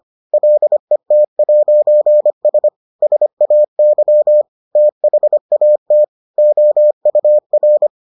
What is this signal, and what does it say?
Morse code, heard as a keyed tone, about 25 wpm, 600 Hz. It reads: LET'S SAY THAT OUR